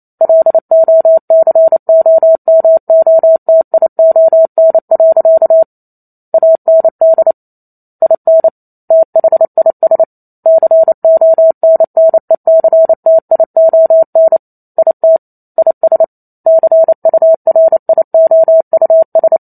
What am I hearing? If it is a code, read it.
Morse code, heard as a keyed tone, about 28 wpm, 650 Hz. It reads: LOCOMOTION. AND IN THIS CONNECTION IT IS CURIOUS
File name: Chapter 2 - 28 WPM